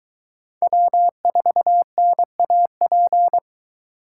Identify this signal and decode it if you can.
Morse code, heard as a keyed tone, about 22 wpm, 700 Hz. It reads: W4NAP